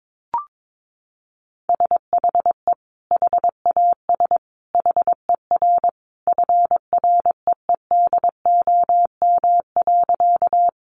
Morse code, heard as a keyed tone, about 22 wpm, 700 Hz. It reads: SHE HAS HER FREEDOM.
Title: Sentences from Top 800 Words 22wpm